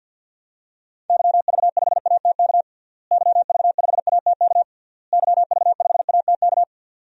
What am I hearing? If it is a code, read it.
Morse code, heard as a keyed tone, about 52 wpm, 700 Hz. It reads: YV5ATK YV5ATK YV5ATK
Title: DXCC-100 Call Signs - 6 Characters - Repeated 3x - 50wpm